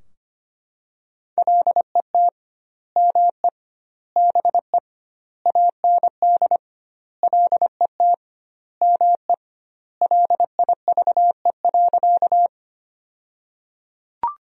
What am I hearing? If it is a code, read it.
Morse code, heard as a keyed tone, about 25 wpm, 700 Hz. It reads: LET ME BE AND LET ME LIVE.
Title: 2x Word-Spacing - Sentences from Top 300 Words - 25wpm